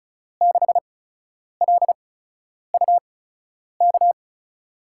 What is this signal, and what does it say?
Morse code, heard as a keyed tone, about 36 wpm, spaced as for 10 wpm, 700 Hz. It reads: 6LUK
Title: USA Call Signs - 6 Characters - 8x Character Spacing - 35wpm